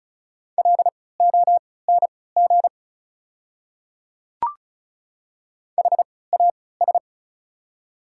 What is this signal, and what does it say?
Morse code, heard as a keyed tone, about 36 wpm, spaced as for 20 wpm, 690 Hz. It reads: LONG HAS
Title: 3x Character Spacing - Top 500 Words - 35wpm